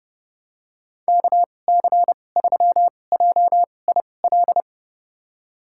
Morse code, heard as a keyed tone, about 30 wpm, spaced as for 22 wpm, 700 Hz. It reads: KC3JIL
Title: USA Call Signs - 6 Characters - 2x Character Spacing - 30wpm